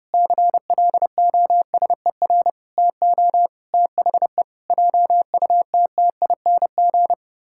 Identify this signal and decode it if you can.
Morse code, heard as a keyed tone, about 30 wpm, 700 Hz. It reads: CLOSER TO THE JUTTING
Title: Pangram Challenge v2 - 30wpm